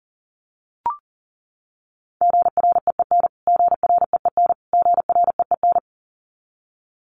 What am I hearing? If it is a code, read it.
Morse code, heard as a keyed tone, about 40 wpm, 700 Hz. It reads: GREEN GREEN GREEN
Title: Top 500 Words Repeated 40wpm